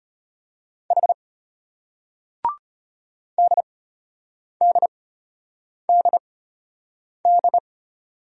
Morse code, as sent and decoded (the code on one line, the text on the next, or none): .... / -.. -.. -.. -..
H DDDD